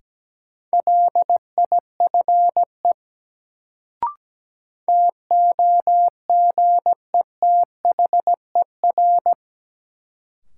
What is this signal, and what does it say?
Morse code, heard as a keyed tone, about 17 wpm, 700 Hz. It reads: LIFE TOGETHER
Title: Sentences from Top 300 Words - New Word Drill - 17wpm